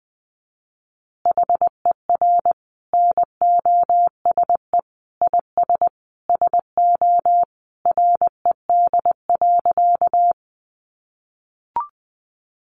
Morse code, heard as a keyed tone, about 19 wpm, 700 Hz. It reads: HER NOSE IS SO RED.
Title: Easy Phrases 20wpm